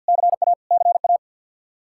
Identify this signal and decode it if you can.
Morse code, heard as a keyed tone, about 52 wpm, 700 Hz. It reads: CA CA
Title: Practice QSO - Exchange 1 - Beginning - 50wpm